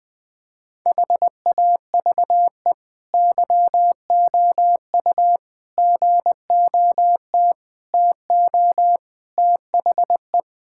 Morse code, heard as a keyed tone, about 20 wpm, 700 Hz. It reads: HAVE YOU GOT TO THE
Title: Sentences from Top 1000 Words - Review Part 6 - 20wpm